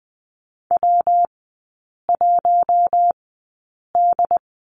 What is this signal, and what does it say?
Morse code, heard as a keyed tone, about 20 wpm, spaced as for 8 wpm, 700 Hz. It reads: W1D
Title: 114 Lesson 19e - Letter w - Three Characters 20wpm